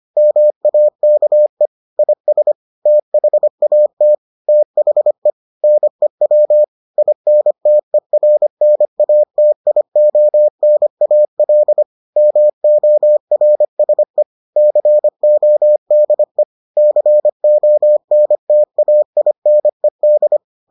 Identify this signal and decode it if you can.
Morse code, heard as a keyed tone, about 25 wpm, 600 Hz. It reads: MAKE IS THAT THE NEW INTERNATIONAL MORSE CODE CONTAINED